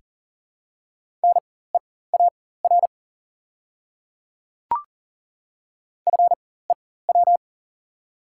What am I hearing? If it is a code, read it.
Morse code, heard as a keyed tone, about 40 wpm, spaced as for 19 wpm, 690 Hz. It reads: NEAR FEW
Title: 4x Character Spacing - Top 300 Words - 40wpm